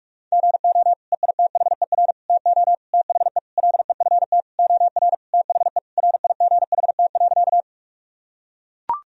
Morse code, heard as a keyed tone, about 41 wpm, 700 Hz. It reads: GO EITHER TO THE LEFT OR THE RIGHT.
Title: Sentences from Top 1000 Words - Review Part 4 - 45wpm